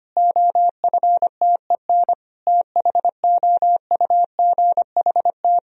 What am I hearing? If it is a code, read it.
Morse code, heard as a keyed tone, about 25 wpm, 700 Hz. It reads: OFTEN THOUGHT
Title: Sentences from Top 1000 Words - Review Part 1 - 25wpm